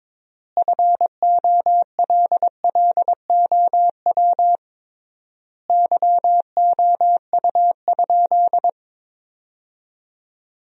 Morse code, heard as a keyed tone, about 22 wpm, 700 Hz. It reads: FOLLOW YOU?
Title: 3x Word-Spacing - Sentences from Top 200 Words - 22wpm